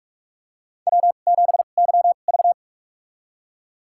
Morse code, heard as a keyed tone, about 43 wpm, spaced as for 32 wpm, 700 Hz. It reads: W7YV
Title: USA Call Signs - 4 Characters - 2x Character Spacing - 45wpm